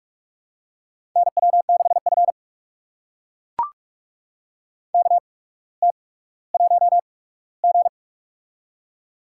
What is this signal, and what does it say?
Morse code, heard as a keyed tone, about 42 wpm, spaced as for 19 wpm, 700 Hz. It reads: NW6F KT1G
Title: USA Call Signs - 4 Characters - 8x Character Spacing - 45wpm